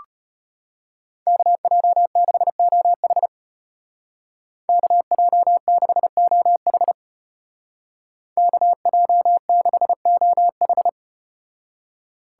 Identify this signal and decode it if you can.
Morse code, heard as a keyed tone, about 34 wpm, 700 Hz. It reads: KJ6OH KJ6OH KJ6OH